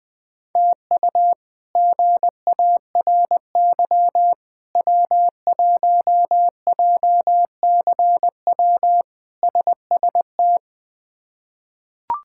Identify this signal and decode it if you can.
Morse code, heard as a keyed tone, about 21 wpm, 700 Hz. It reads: TU GARY W1JCW SST
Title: K1USN SST QSO Practice - 20wpm